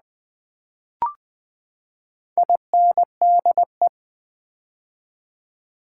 Morse code, heard as a keyed tone, about 20 wpm, 700 Hz. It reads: INDE
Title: Build A Word - 4 Letters At A Time - 11-Letter Words - 20wpm